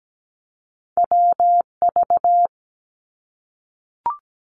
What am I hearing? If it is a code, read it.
Morse code, heard as a keyed tone, about 17 wpm, 700 Hz. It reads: WV
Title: US States Repeated 3x 17wpm